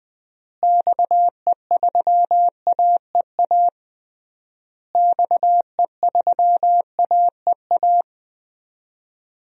Morse code, heard as a keyed tone, about 21 wpm, 700 Hz. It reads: XE3AEA XE3AEA
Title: DXCC-100 Call Signs - 6 Characters - Repeated 3x - 20wpm